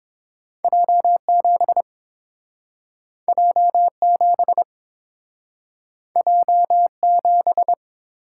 Morse code, heard as a keyed tone, about 29 wpm, 700 Hz. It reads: J7 J7 J7